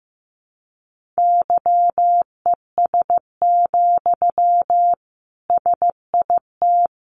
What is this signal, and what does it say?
Morse code, heard as a keyed tone, about 15 wpm, 700 Hz. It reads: YES, SIT
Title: Sentences from Top 500 Words 15wpm